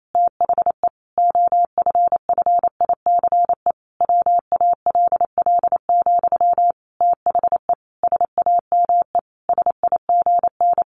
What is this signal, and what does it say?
Morse code, heard as a keyed tone, about 28 wpm, 700 Hz. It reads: THE OFFICE WALL, THE SAME SIGN
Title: A Sound Of Thunder 28wpm